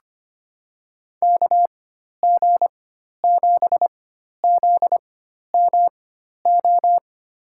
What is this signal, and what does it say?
Morse code, heard as a keyed tone, about 25 wpm, spaced as for 12 wpm, 700 Hz. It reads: KG7ZMO
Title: USA Call Signs - 6 Characters - 4x Character Spacing - 25wpm